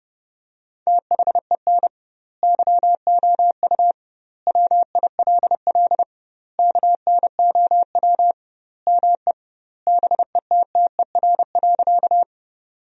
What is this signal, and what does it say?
Morse code, heard as a keyed tone, about 32 wpm, 700 Hz. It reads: THEN YOU WILL KNOW ME BETTER.
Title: 2x Word-Spacing - Sentences from Top 400 Words - 30wpm